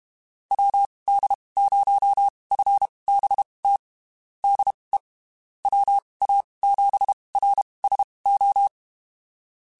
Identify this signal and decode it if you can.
Morse code, heard as a keyed tone, about 32 wpm, spaced as for 23 wpm, 800 Hz. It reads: WD0FBT DE WA7RSO
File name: (spaced) / QSOs Spaced-mp3s / WD0FBT 1 @30 sp